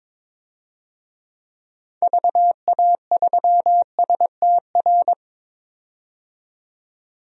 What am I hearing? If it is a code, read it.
Morse code, heard as a keyed tone, about 22 wpm, 700 Hz. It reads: VA3STR
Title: DXCC-100 Call Signs - 6 Characters - 22wpm